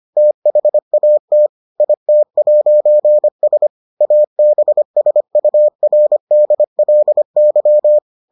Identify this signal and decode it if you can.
Morse code, heard as a keyed tone, about 25 wpm, 600 Hz. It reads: THAT IT'S ABSURDLY